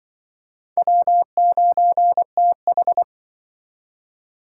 Morse code, heard as a keyed tone, about 25 wpm, 700 Hz. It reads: W9TH